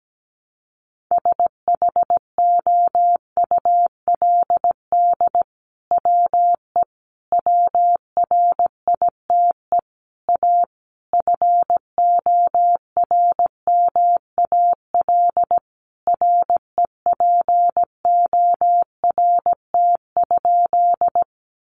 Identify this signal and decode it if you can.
Morse code, heard as a keyed tone, about 17 wpm, 700 Hz. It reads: SHOULD WE WRITE A FORMAL REPORT?
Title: Sentences from Top 1300 Words 17wpm